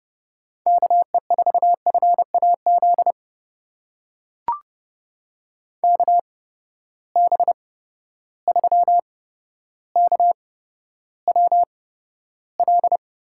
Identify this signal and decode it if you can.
Morse code, heard as a keyed tone, about 29 wpm, spaced as for 13 wpm, 700 Hz. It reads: KE4FAZ KB3KWL
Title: USA Call Signs - 6 Characters - 8x Character Spacing - 30wpm